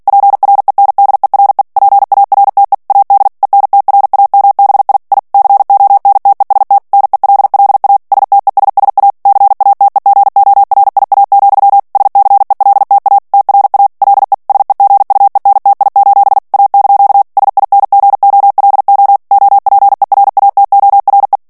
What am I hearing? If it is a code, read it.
Morse code, heard as a keyed tone, about 48 wpm, 800 Hz. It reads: PRENDERE PARTE AD ENTRAMBI I CONTEST NELLA STESSA CATEGORIA, SCELTA TRA LE SEGUENTI: A) SINGOLO OPERATORE
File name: ITALIEN texte clair 2 vitesse 45 mots minute